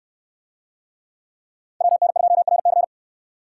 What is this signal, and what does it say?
Morse code, heard as a keyed tone, about 68 wpm, 700 Hz. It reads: WN2RC